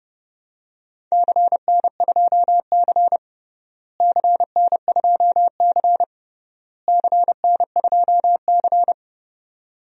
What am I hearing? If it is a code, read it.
Morse code, heard as a keyed tone, about 30 wpm, 700 Hz. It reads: CN2C CN2C CN2C
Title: DXCC-100 Call Signs - 4 Characters - Repeated 3x - 30wpm